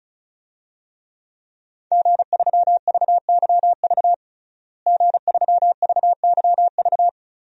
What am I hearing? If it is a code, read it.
Morse code, heard as a keyed tone, about 34 wpm, 700 Hz. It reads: G3VYV G3VYV
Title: DXCC-100 Call Signs - 5 Characters - Repeated 3x - 35wpm